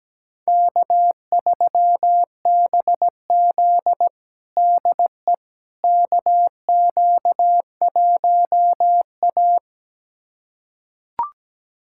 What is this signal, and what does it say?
Morse code, heard as a keyed tone, about 17 wpm, 700 Hz. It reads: K3BZ DE KQ1A
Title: USA Callsign DE Callsign - 4 Characters - 17wpm